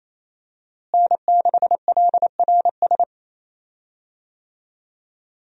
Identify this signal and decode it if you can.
Morse code, heard as a keyed tone, about 28 wpm, 700 Hz. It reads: N6LRS
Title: USA Call Signs - 5 Characters - Repeated 3x - 28wpm